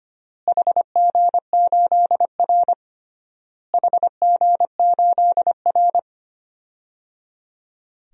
Morse code, heard as a keyed tone, about 25 wpm, 700 Hz. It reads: HG8R HG8R
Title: DXCC-100 Call Signs - 4 Characters - Repeated 3x - 25wpm